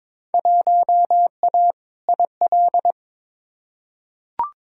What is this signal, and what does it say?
Morse code, heard as a keyed tone, about 22 wpm, 700 Hz. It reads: 1A IL
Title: Field Day Exchanges - 22wpm